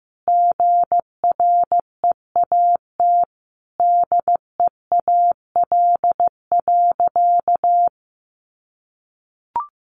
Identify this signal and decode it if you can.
Morse code, heard as a keyed tone, about 15 wpm, 700 Hz. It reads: GREAT DEAL.
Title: Sentences from Top 800 Words 15wpm